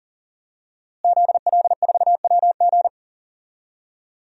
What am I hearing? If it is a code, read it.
Morse code, heard as a keyed tone, about 40 wpm, 700 Hz. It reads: ZL4WG
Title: DXCC-100 Call Signs - 5 Characters - Repeated 3x - 40wpm